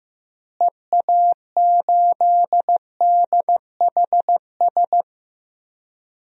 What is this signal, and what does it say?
Morse code, heard as a keyed tone, about 15 wpm, 700 Hz. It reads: EA8DHS
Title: DXCC-100 Call Signs - 6 Characters - Repeated 2x - 15wpm